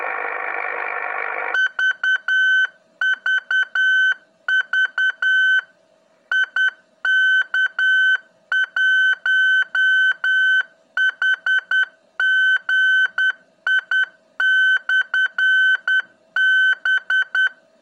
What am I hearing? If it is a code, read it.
Morse code, heard as a keyed tone, about 10 wpm, 1550 Hz. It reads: VVV IK1HGI/B
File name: ik1hgi / utiliti / ik1hgi-qrss